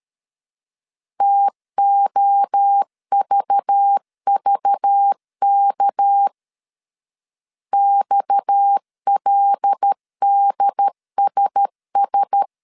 Morse code, heard as a keyed tone, about 13 wpm, 800 Hz. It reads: TOVVK XLDSS
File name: CW / DET / 1 9 1500C S C 12wpm 3811